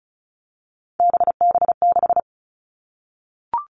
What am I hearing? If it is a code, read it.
Morse code, heard as a keyed tone, about 36 wpm, 700 Hz. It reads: BB6
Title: Three Challenging Letter-Number Combinations 35wpm